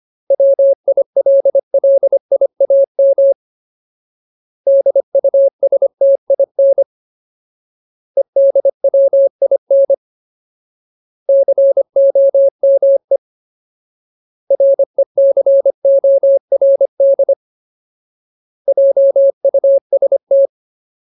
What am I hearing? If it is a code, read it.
Morse code, heard as a keyed tone, about 25 wpm, 550 Hz. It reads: WILLIAM DUSTIN EDWIN COME RECORD JUST
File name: Ultimate Word List / 25WPM Ultimate